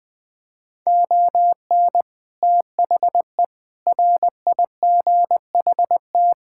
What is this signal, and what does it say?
Morse code, heard as a keyed tone, about 21 wpm, 700 Hz. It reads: ON THE RIGHT
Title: Sets of 6 Words - 20wpm